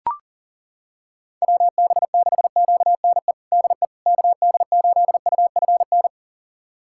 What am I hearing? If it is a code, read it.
Morse code, heard as a keyed tone, about 40 wpm, 700 Hz. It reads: WB6QNE DE KD8UFN
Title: USA Callsign DE Callsign - 6 Characters - 40wpm